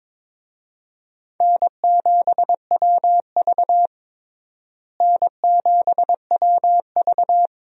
Morse code, heard as a keyed tone, about 22 wpm, 700 Hz. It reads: N7WV N7WV
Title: USA Call Signs - 4 Characters - Repeated 2x - 22wpm